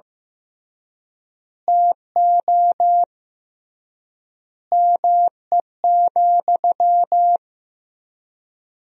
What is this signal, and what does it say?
Morse code, heard as a keyed tone, about 16 wpm, 700 Hz. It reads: TO ME,
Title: 3x Word-Spacing - Sentences from Top 500 Words - 15wpm